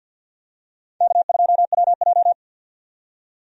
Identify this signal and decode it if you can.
Morse code, heard as a keyed tone, about 52 wpm, 700 Hz. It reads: K2WJ
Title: USA Call Signs - 4 Characters - 50wpm